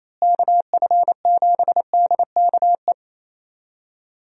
Morse code, heard as a keyed tone, about 27 wpm, 700 Hz. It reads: KF7DKE
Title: USA Call Signs - 6 Characters - Repeated 3x - 28wpm